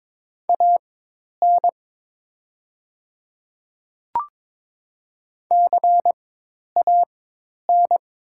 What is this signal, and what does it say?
Morse code, heard as a keyed tone, about 22 wpm, spaced as for 10 wpm, 700 Hz. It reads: AN CAN
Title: 4x Character Spacing - Top 100 Words - 22wpm